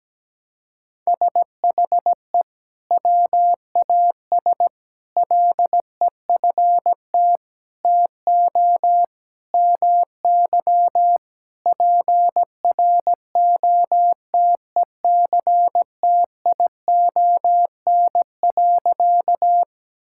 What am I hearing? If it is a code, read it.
Morse code, heard as a keyed tone, about 17 wpm, 700 Hz. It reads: SHE WAS LEFT TO MY PROTECTION.
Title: Sentences from Top 1600 Words 17wpm